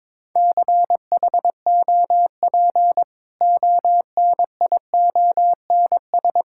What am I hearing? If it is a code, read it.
Morse code, heard as a keyed tone, about 22 wpm, 700 Hz. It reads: CHOP ONIONS
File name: practice / 22wpm / Easy Sentences with Hints - 22wpm